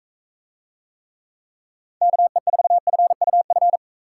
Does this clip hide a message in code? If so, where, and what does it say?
Yes: Morse code from 2.0 to 3.8 s, heard as a keyed tone, about 40 wpm, 700 Hz. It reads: KE4FUF